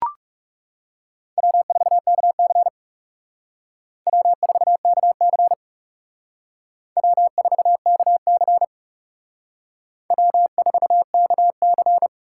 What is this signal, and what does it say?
Morse code, heard as a keyed tone, about 37 wpm, 700 Hz. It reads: W4KC W4KC W4KC W4KC
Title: Speed Racing - USA 1x2 Call Signs 30wpm